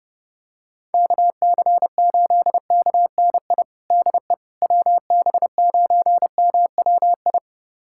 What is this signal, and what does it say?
Morse code, heard as a keyed tone, about 32 wpm, 700 Hz. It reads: KC8KNI DE WB9MWI
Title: USA Callsign DE Callsign 30wpm